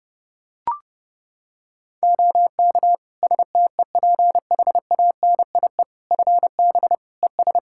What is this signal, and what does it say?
Morse code, heard as a keyed tone, about 32 wpm, 700 Hz. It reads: OK STEPHANIE FB ES
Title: Practice QSO - Exchange 2 - Info - 30wpm